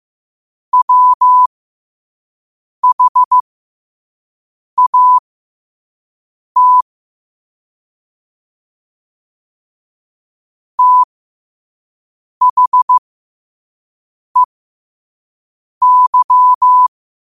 Morse code, heard as a keyed tone, about 15 wpm, spaced as for 5 wpm, 1000 Hz. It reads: WHAT THEY
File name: qotdfiles / 5wpm